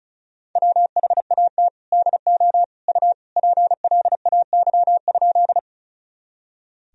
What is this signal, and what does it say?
Morse code, heard as a keyed tone, about 36 wpm, 700 Hz. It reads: WHAT DO U PLAY?
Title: Sentences from Top 200 Words and Abbreviations 35wpm